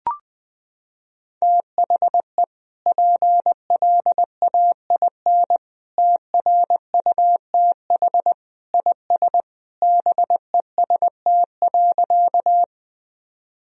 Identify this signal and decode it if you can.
Morse code, heard as a keyed tone, about 21 wpm, 700 Hz. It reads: THE PLAIN TRUTH IS BEST.
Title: Sentences from Top 1000 Words - Review Part 2 - 20wpm